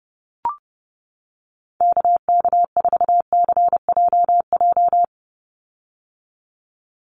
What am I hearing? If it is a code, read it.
Morse code, heard as a keyed tone, about 30 wpm, 700 Hz. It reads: KK4CJJ
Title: USA Call Signs 30wpm